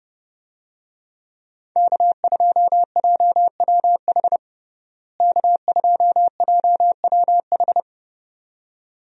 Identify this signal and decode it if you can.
Morse code, heard as a keyed tone, about 32 wpm, 700 Hz. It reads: K2JWH K2JWH
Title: USA Call Signs - 5 Characters - Repeated 2x - 30wpm